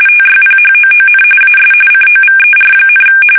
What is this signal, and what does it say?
RTTY, about 45.45 baud. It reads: TYPHOON AS COVER,
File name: fsk850 / battleshipnj